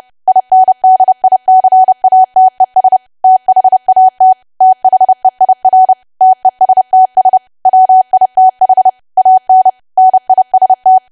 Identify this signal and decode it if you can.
Morse code, heard as a keyed tone, about 30 wpm, 750 Hz. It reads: INDICATES THAT THEIR TESTS WITH AN NIST
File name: cwa / W1AW-Level3 / 30WPM